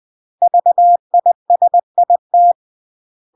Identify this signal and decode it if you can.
Morse code, heard as a keyed tone, about 20 wpm, 700 Hz. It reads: VISIT
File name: CW / jodorowsky / 82 commandments 20-20